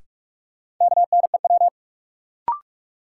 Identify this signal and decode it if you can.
Morse code, heard as a keyed tone, about 46 wpm, 700 Hz. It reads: KNEW
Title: Sentences from Top 400 Words - New Word Drill - 45wpm